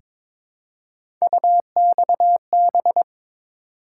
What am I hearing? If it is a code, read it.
Morse code, heard as a keyed tone, about 22 wpm, 700 Hz. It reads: UXB
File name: practice / 22wpm / Three Letter-Number Combinations - 22wpm